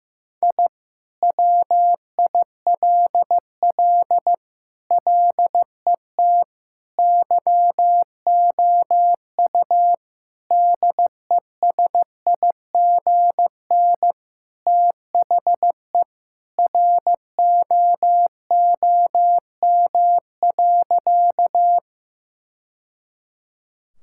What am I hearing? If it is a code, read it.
Morse code, heard as a keyed tone, about 16 wpm, 700 Hz. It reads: I WILL LET YOU DESIGN THE ROOM.
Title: Sentences from Top 1000 Words - Review Part 5 - 15wpm